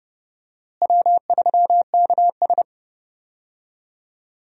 Morse code, heard as a keyed tone, about 32 wpm, 700 Hz. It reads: W3KS
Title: USA Call Signs - 4 Characters - Repeated 3x - 30wpm